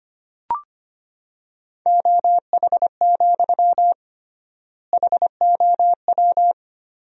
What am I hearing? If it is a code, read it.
Morse code, heard as a keyed tone, about 25 wpm, 700 Hz. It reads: OH, HOW
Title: 3x Word-Spacing - Sentences from Top 500 Words - 25wpm